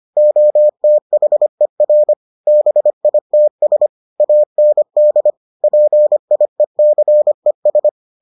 Morse code, heard as a keyed tone, about 25 wpm, 600 Hz. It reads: OTHER BITS AND PIECES